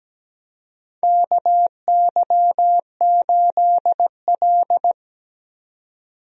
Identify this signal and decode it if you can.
Morse code, heard as a keyed tone, about 17 wpm, 700 Hz. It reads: KY8L
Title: USA Call Signs - 4 Characters - 17wpm